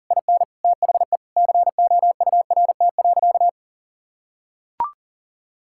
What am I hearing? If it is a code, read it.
Morse code, heard as a keyed tone, about 40 wpm, 700 Hz. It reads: IN THE COURT.
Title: Sentences from Top 1600 Words 40wpm